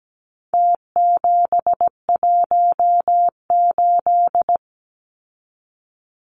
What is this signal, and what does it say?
Morse code, heard as a keyed tone, about 17 wpm, 700 Hz. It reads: T718
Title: Numbers - Contest Style - 4-Digit 17wpm